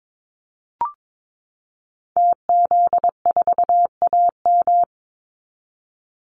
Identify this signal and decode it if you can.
Morse code, heard as a keyed tone, about 22 wpm, 700 Hz. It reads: TZ4AM
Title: Call Signs - 22wpm